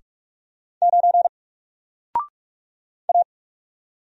Morse code, heard as a keyed tone, about 44 wpm, 700 Hz. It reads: 9 A
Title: Single Character (Letter:Number:Punctuation:Prosign) Alt 45wpm